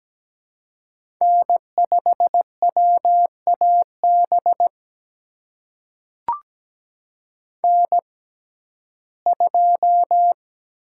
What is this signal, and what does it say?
Morse code, heard as a keyed tone, about 17 wpm, spaced as for 12 wpm, 700 Hz. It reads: N5WAB N2
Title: USA Call Signs - 5 Characters - 6x Character Spacing - 17wpm